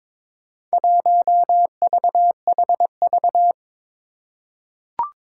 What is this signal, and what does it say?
Morse code, heard as a keyed tone, about 22 wpm, 700 Hz. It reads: 1VHV
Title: Four Challenging Letter-Number Combinations 22wpm